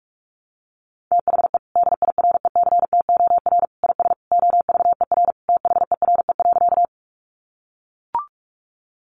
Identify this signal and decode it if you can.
Morse code, heard as a keyed tone, about 44 wpm, 700 Hz. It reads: THE DIRECTOR IS OVER THERE.
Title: Sentences from Top 1300 Words 45wpm